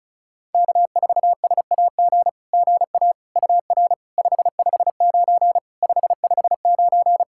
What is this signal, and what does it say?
Morse code, heard as a keyed tone, about 36 wpm, 690 Hz. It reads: K4SAG GA UR 559 559
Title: SOTA QSO Practice 35wpm